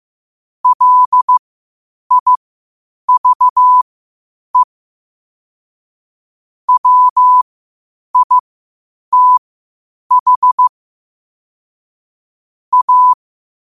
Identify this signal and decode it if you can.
Morse code, heard as a keyed tone, about 15 wpm, spaced as for 8 wpm, 1000 Hz. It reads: LIVE WITH A
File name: qotdfiles / 8wpm